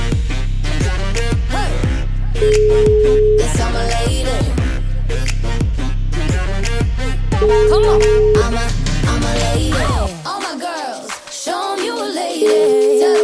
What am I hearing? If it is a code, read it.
Ringback tone (Europe, 425 Hz), with noise.